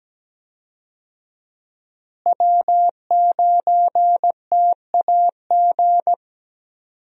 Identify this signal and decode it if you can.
Morse code, heard as a keyed tone, about 17 wpm, 700 Hz. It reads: W9TAG
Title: USA Call Signs - 5 Characters - Repeated 3x - 17wpm